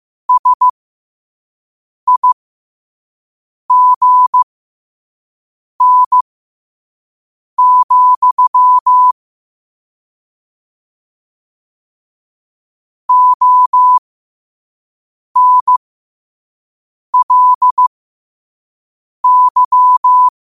Morse code, heard as a keyed tone, about 15 wpm, spaced as for 5 wpm, 1000 Hz. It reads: SIGN, ONLY
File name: qotdfiles / 5wpm